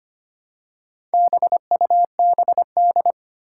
Morse code, heard as a keyed tone, about 25 wpm, 700 Hz. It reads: BUBD
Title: Four Challenging Letter-Number Combinations 25wpm